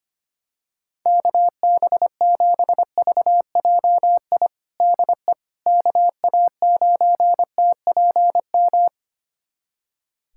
Morse code, heard as a keyed tone, about 25 wpm, 700 Hz. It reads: KB7VJI DE KA9TPM
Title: USA Callsign DE Callsign - 6 Characters - 25wpm